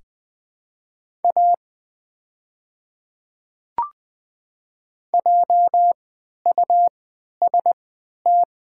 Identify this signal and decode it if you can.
Morse code, heard as a keyed tone, about 20 wpm, spaced as for 11 wpm, 700 Hz. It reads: A JUST
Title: 3x Character Spacing - Top 300 Words - 20wpm